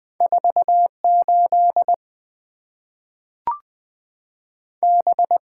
Morse code, heard as a keyed tone, about 20 wpm, 700 Hz. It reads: 48 B
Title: Two Letter-Number Combinations - 20wpm